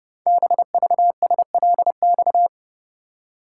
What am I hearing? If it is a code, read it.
Morse code, heard as a keyed tone, about 30 wpm, 700 Hz. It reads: BVSLX
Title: Code Groups - 5 Letter:Numbers - 30wpm